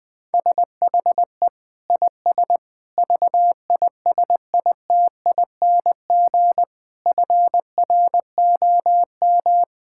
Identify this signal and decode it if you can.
Morse code, heard as a keyed tone, about 21 wpm, 700 Hz. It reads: SHE IS VISITING FROM
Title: Sentences with 4-Word Phrases - 20wpm